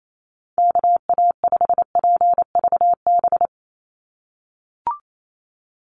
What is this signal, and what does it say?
Morse code, heard as a keyed tone, about 28 wpm, 700 Hz. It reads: KA5PVB
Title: NA SOTA Chaser Callsigns 28wpm